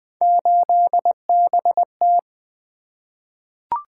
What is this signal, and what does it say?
Morse code, heard as a keyed tone, about 20 wpm, 700 Hz. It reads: 8BT
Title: Three Letter-Number Combinations - 20wpm